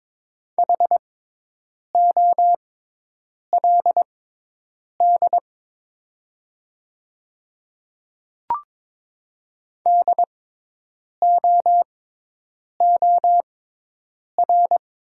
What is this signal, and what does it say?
Morse code, heard as a keyed tone, about 22 wpm, spaced as for 8 wpm, 700 Hz. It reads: HOLD DOOR
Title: 6x Character Spacing - Top 500 Words - 22wpm